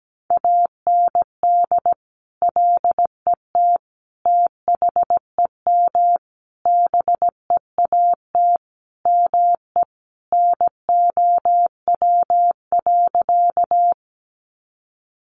Morse code, heard as a keyed tone, about 17 wpm, 700 Hz. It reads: AND LET THEM BEAT ME NOW.
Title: Sentences from Top 1300 Words 17wpm